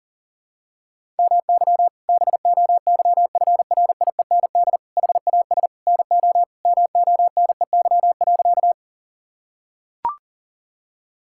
Morse code, heard as a keyed tone, about 37 wpm, 700 Hz. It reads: MY BOYFRIEND HAS NO MONEY.
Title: Sentences from Top 1000 Words - Review Part 5 - 40wpm